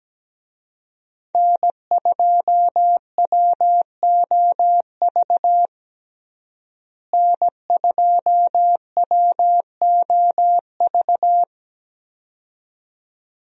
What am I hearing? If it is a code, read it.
Morse code, heard as a keyed tone, about 17 wpm, 700 Hz. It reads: N2WOV N2WOV
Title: USA Call Signs - 5 Characters - Repeated 2x - 17wpm